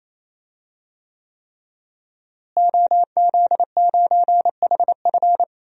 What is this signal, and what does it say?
Morse code, heard as a keyed tone, about 28 wpm, 700 Hz. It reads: OZ9HF
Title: DXCC-100 Call Signs - 5 Characters - Repeated 3x - 28wpm